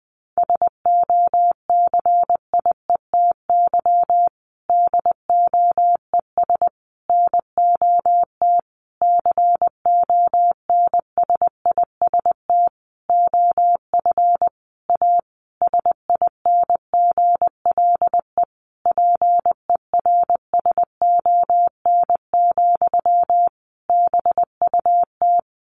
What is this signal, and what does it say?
Morse code, heard as a keyed tone, about 19 wpm, 700 Hz. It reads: SOCIETY DOES NOT CONSIST OF A SINGLE PERSON, BUT
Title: Sentences from Top 1400 Words 20wpm